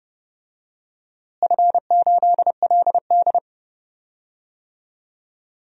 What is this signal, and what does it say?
Morse code, heard as a keyed tone, about 30 wpm, 700 Hz. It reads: F8LD
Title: DXCC-100 Call Signs - 4 Characters - Repeated 2x - 30wpm